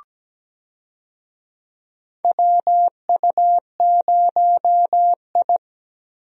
Morse code, heard as a keyed tone, about 17 wpm, 700 Hz. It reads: WU0I
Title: USA Call Signs - 4 Characters - Repeated 3x - 17wpm